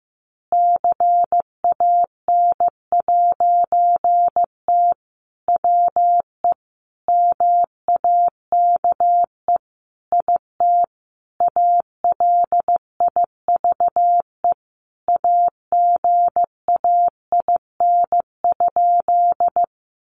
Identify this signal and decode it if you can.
Morse code, heard as a keyed tone, about 15 wpm, 700 Hz. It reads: CAN'T WE MAKE IT ALIVE AGAIN?
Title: A Sound Of Thunder 15wpm